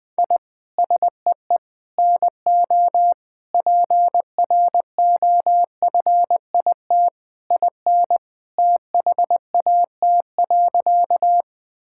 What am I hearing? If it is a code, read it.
Morse code, heard as a keyed tone, about 21 wpm, 700 Hz. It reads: I SEE NO PROFIT IN THAT.
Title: Sentences from Top 1000 Words - Review Part 5 - 20wpm